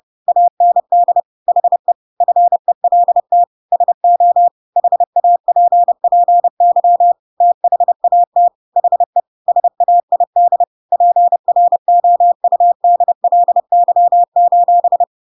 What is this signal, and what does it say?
Morse code, heard as a keyed tone, about 30 wpm, 700 Hz. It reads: AND HE FELT SO HAPPY THAT HE SAID PROUDLY:
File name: Pinocchio / Pinocchio ch8 30 30 ik1hge